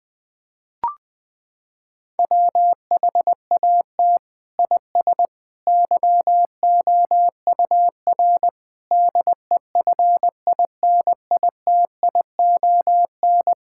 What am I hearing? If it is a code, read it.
Morse code, heard as a keyed tone, about 20 wpm, 700 Hz. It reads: WHAT IS YOUR DEFINITION
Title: Sentences from Top 1000 Words - Review Part 6 - 20wpm